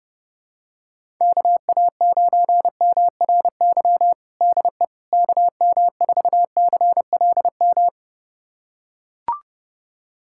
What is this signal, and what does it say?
Morse code, heard as a keyed tone, about 32 wpm, 700 Hz. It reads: KA9MRY DE KM4CLM
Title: USA Callsign DE Callsign - 6 Characters - 30wpm